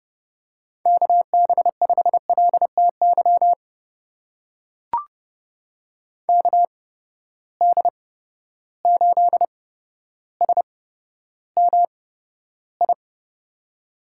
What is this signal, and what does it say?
Morse code, heard as a keyed tone, about 30 wpm, spaced as for 13 wpm, 700 Hz. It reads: KB5LTY KD8SMI
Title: USA Call Signs - 6 Characters - 8x Character Spacing - 30wpm